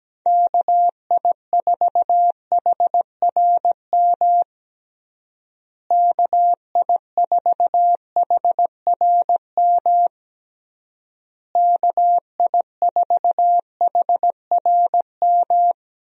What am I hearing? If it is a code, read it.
Morse code, heard as a keyed tone, about 17 wpm, 700 Hz. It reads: KI4HRM KI4HRM KI4HRM
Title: USA Call Signs - 6 Characters - Repeated 3x - 17wpm